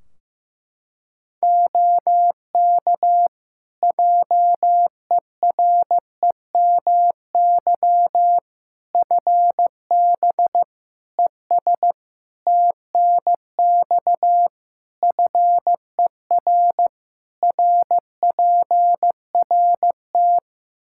Morse code, heard as a keyed tone, about 16 wpm, 700 Hz. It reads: OK JEREMY FB ES TNX FER RPRT
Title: Practice QSO - Exchange 2 - Info - 15wpm